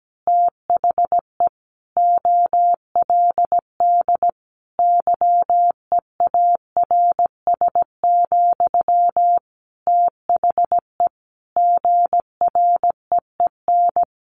Morse code, heard as a keyed tone, about 17 wpm, 700 Hz. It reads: THE OLD YEARS, THE GREEN
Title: A Sound Of Thunder 17wpm